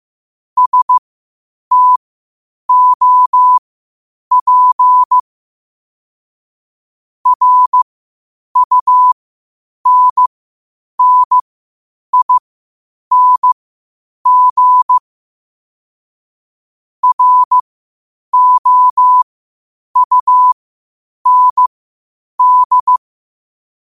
Morse code, heard as a keyed tone, about 15 wpm, spaced as for 8 wpm, 1000 Hz. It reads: STOP RUNNING ROUND
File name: qotdfiles / 8wpm